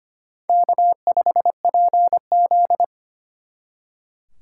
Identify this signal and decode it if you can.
Morse code, heard as a keyed tone, about 32 wpm, 700 Hz. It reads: K5PZ